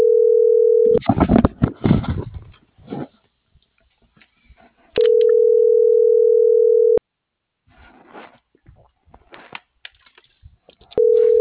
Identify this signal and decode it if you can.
Ringback tone (North America), 440 + 480 Hz.